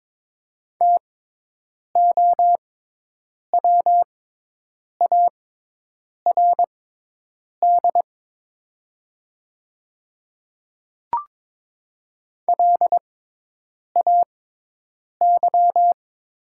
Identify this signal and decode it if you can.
Morse code, heard as a keyed tone, about 22 wpm, spaced as for 8 wpm, 700 Hz. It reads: TOWARD LAY
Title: 6x Character Spacing - Top 500 Words - 22wpm